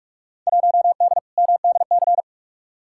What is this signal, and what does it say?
Morse code, heard as a keyed tone, about 46 wpm, 700 Hz. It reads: 1D MDC
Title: Field Day Exchanges - 45wpm